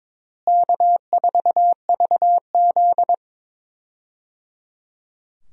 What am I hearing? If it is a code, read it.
Morse code, heard as a keyed tone, about 22 wpm, 700 Hz. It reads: K4VZ
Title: USA Call Signs - 4 Characters - Repeated 3x - 22wpm